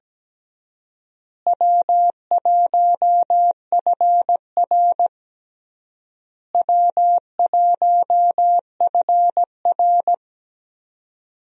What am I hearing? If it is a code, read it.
Morse code, heard as a keyed tone, about 17 wpm, 700 Hz. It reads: W1FR W1FR
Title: USA Call Signs - 4 Characters - Repeated 3x - 17wpm